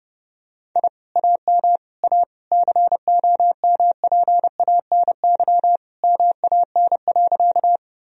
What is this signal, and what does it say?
Morse code, heard as a keyed tone, about 30 wpm, 700 Hz. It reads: I AM A COMPANY MAN.
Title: Sentences from Top 1000 Words - Review Part 3 - 30wpm